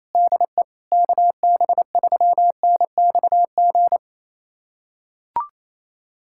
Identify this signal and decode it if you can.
Morse code, heard as a keyed tone, about 27 wpm, 700 Hz. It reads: DE KB3NXG
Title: USA Callsign DE Callsign - 6 Characters - 28wpm